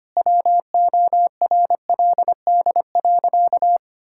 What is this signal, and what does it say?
Morse code, heard as a keyed tone, about 25 wpm, 700 Hz. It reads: WORLD.
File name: practice / 25wpm / Sentences with 3-Word Phrases - 25wpm